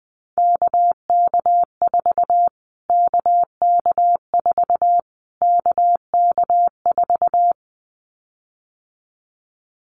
Morse code, heard as a keyed tone, about 20 wpm, 700 Hz. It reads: KK4 KK4 KK4
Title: Callsign Prefixes for Top 25 DXCC - Repeated 3x - 20wpm